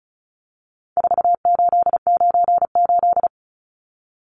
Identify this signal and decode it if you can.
Morse code, heard as a keyed tone, about 34 wpm, 700 Hz. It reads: 4898
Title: Numbers - 4-Digit 35wpm